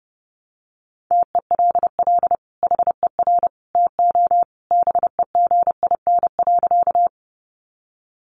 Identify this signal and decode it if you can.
Morse code, heard as a keyed tone, about 30 wpm, 700 Hz. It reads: TELL HER TO BEGIN.
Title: Sentences from Top 300 Words 30wpm